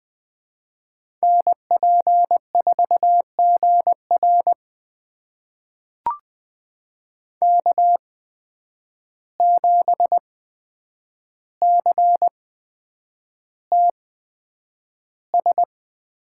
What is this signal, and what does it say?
Morse code, heard as a keyed tone, about 20 wpm, spaced as for 9 wpm, 700 Hz. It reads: NP4GR K7CTS
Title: USA Call Signs - 5 Characters - 8x Character Spacing - 20wpm